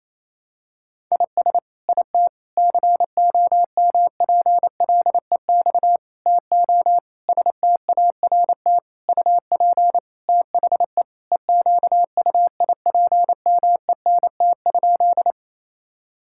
Morse code, heard as a keyed tone, about 28 wpm, 700 Hz. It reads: IS IT COMPLEX TO START UP THE EQUIPMENT?
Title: Sentences from Top 1000 Words - Review Part 6 - 28wpm